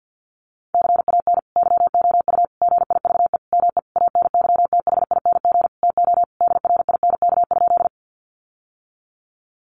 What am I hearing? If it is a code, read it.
Morse code, heard as a keyed tone, about 50 wpm, 700 Hz. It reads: CAN YOU GIVE ME ANYTHING TO DRINK?
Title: Sentences from Top 1500 Words 50wpm